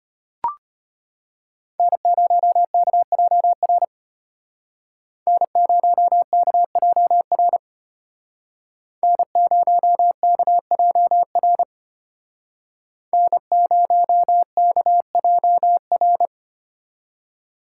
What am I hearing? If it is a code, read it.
Morse code, heard as a keyed tone, about 32 wpm, 700 Hz. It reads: N0KJR N0KJR N0KJR N0KJR